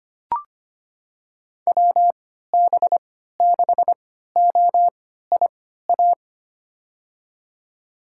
Morse code, heard as a keyed tone, about 25 wpm, spaced as for 14 wpm, 700 Hz. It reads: WB6OIA
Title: USA Call Signs - 6 Characters - 3x Character Spacing - 25wpm